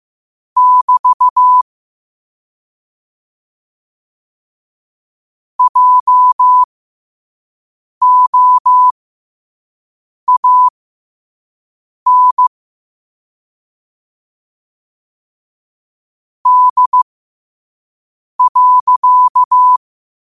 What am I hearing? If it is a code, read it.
Morse code, heard as a keyed tone, about 15 wpm, spaced as for 5 wpm, 1000 Hz. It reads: <BT> JOAN D.